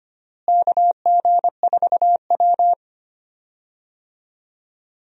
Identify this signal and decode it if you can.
Morse code, heard as a keyed tone, about 25 wpm, 700 Hz. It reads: KG4W
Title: USA Call Signs - 4 Characters - Repeated 3x - 25wpm